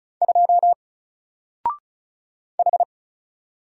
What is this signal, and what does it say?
Morse code, heard as a keyed tone, about 36 wpm, 700 Hz. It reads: 2 H
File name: Single-Letter-Number-35wpm